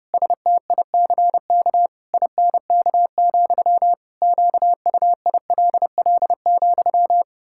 Morse code, heard as a keyed tone, about 30 wpm, 700 Hz. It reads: STICK INK, QUILL,
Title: Pangram Challenge v2 - 30wpm